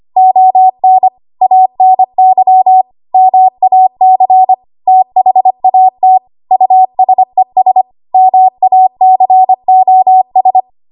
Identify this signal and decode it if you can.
Morse code, heard as a keyed tone, about 25 wpm, 750 Hz. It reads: ON ANY MAC THAT USES MACOS
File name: Archive / 25 WPM / 25WPM